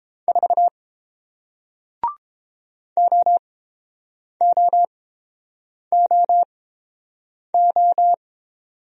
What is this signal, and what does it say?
Morse code, heard as a keyed tone, about 31 wpm, spaced as for 8 wpm, 700 Hz. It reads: 4 OOOO